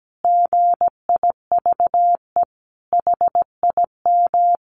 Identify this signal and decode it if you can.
Morse code, heard as a keyed tone, about 17 wpm, 700 Hz. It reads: GIVE HIM
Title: Sentences from Top 1500 Words 17wpm